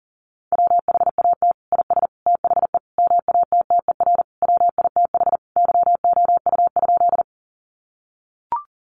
Morse code, heard as a keyed tone, about 40 wpm, 700 Hz. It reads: WHAT IS THE MATTER WITH YOU?
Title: Sentences from Top 700 Words 40wpm